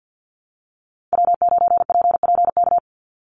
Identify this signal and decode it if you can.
Morse code, heard as a keyed tone, about 50 wpm, 700 Hz. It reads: W9PPK
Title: USA Call Signs 50wpm